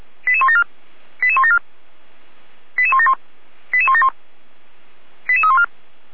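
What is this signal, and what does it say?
Five-tone selective call (ZVEI), heard as quick runs of five tones, tones of about 70 ms. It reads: 88174 88174 88171 88171 88314
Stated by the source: ZVEI five-tone call received by SDR, archived by fire brigade Feuerwehr Orlamünde.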